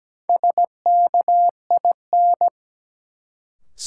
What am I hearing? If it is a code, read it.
Morse code, heard as a keyed tone, about 17 wpm, 700 Hz. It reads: SKIN
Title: Sentences from Top 1300 Words - New Word Drill - 17wpm